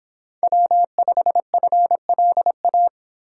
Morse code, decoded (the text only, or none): W5FLA